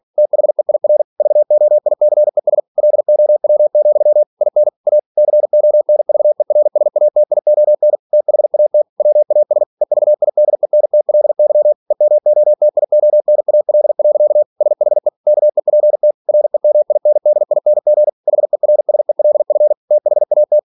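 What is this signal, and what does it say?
Morse code, heard as a keyed tone, about 46 wpm, 600 Hz. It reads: THEIR VOICES LOW, IN A CONVERSATION THAT WAS EVIDENTLY EMOTIONAL. SHE KEPT REMINDING HERSELF THAT
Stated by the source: Practice text from an English novel sent at 47 WPM.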